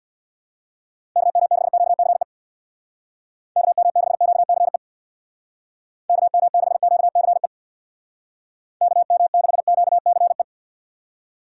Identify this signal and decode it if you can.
Morse code, heard as a keyed tone, about 65 wpm, 700 Hz. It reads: KM6QCE KM6QCE KM6QCE KM6QCE